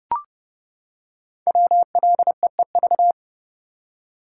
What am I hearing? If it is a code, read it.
Morse code, heard as a keyed tone, about 30 wpm, 700 Hz. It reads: WLEEV
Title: Code Groups - 5 Letters - 30wpm